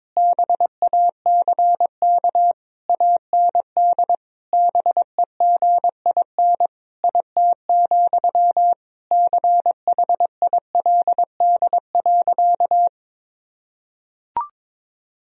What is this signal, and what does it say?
Morse code, heard as a keyed tone, about 22 wpm, 700 Hz. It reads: BACK AND BEGIN IT, CHILD.
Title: Sentences from Top 1000 Words - Review Part 1 - 22wpm